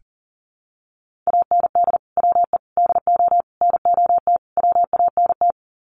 Morse code, heard as a keyed tone, about 40 wpm, 700 Hz. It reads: AND WE DO NOT WANT
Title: Sets of 5 Words Encore2 40wpm